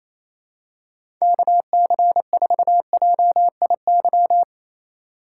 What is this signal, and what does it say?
Morse code, heard as a keyed tone, about 27 wpm, 700 Hz. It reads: KC4JIY
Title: USA Call Signs - 6 Characters - 2x Character Spacing - 28wpm